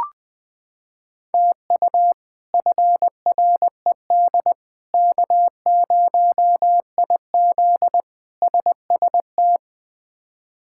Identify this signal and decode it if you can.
Morse code, heard as a keyed tone, about 20 wpm, 700 Hz. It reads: TU FRED K0IZ SST
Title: K1USN SST QSO Practice - 20wpm